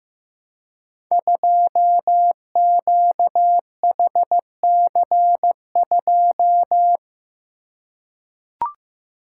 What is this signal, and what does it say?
Morse code, heard as a keyed tone, about 15 wpm, 700 Hz. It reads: 2QHC2
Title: Code Groups - 5 Letter:Numbers - 15wpm